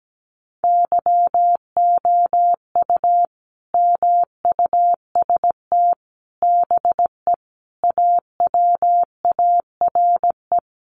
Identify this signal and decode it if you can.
Morse code, heard as a keyed tone, about 17 wpm, 700 Hz. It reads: YOU MUST BE AWARE
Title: Sentences from Top 1200 Words 17wpm